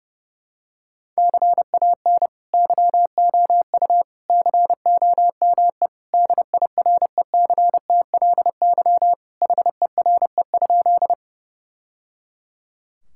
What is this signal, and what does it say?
Morse code, heard as a keyed tone, about 31 wpm, 700 Hz. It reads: CAN YOU COME DIRECTLY HERE?
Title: Sentences from Top 1000 Words - Review Part 5 - 30wpm